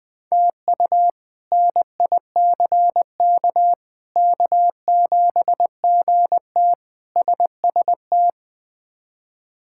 Morse code, heard as a keyed tone, about 20 wpm, 700 Hz. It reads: TU NICK K7GT SST
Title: K1USN SST QSO Practice - 20wpm